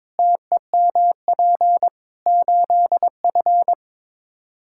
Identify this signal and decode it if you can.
Morse code, heard as a keyed tone, about 21 wpm, 700 Hz. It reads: TEMP 8F
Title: Practice QSO - Exchange 2 - Info - 22wpm